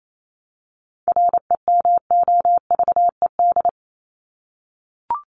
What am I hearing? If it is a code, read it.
Morse code, heard as a keyed tone, about 28 wpm, 700 Hz. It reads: REMOVED
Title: ED Suffix - Top 100 Words with Base Words 28wpm